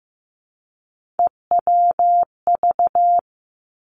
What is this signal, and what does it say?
Morse code, heard as a keyed tone, about 15 wpm, 700 Hz. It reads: EWV
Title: Three Challenging Letter-Number Combinations 15wpm